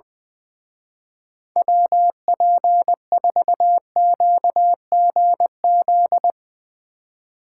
Morse code, heard as a keyed tone, about 20 wpm, 700 Hz. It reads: WP4QGZ
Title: Speed Racing - USA Call Signs 20wpm